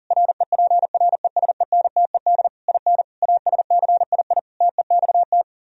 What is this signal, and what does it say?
Morse code, heard as a keyed tone, about 51 wpm, 700 Hz. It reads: REPRESENTED IN ASCII TEXT